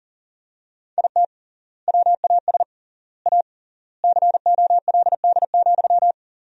Morse code, heard as a keyed tone, about 40 wpm, 700 Hz. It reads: IT WAS A COLD,
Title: 3x Word-Spacing - Sentences from Top 500 Words - 40wpm